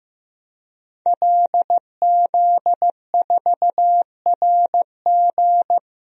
Morse code, heard as a keyed tone, about 15 wpm, 700 Hz. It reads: LZ4RG
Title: DXCC-100 Call Signs - 5 Characters - 15wpm